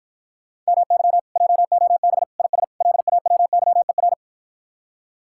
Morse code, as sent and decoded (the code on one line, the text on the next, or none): -- -.-- / .--- --- -... / .. ... / .-.. .- .-- -.-- . .-.
MY JOB IS LAWYER